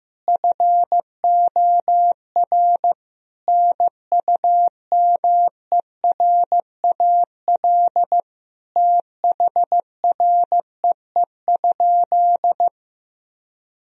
Morse code, heard as a keyed tone, about 15 wpm, 700 Hz. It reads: FOR NUMERAL THREE?
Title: Sentences from Top 1000 Words - Review Part 2 - 15wpm